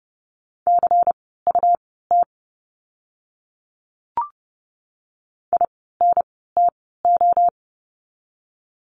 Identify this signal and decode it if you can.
Morse code, heard as a keyed tone, about 30 wpm, spaced as for 17 wpm, 700 Hz. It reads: CUT INTO
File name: Lesson 17f - Letter u - Words 30wpm